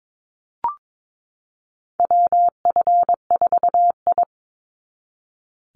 Morse code, heard as a keyed tone, about 22 wpm, 700 Hz. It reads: WF4I
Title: NA SOTA Chaser Callsigns 22wpm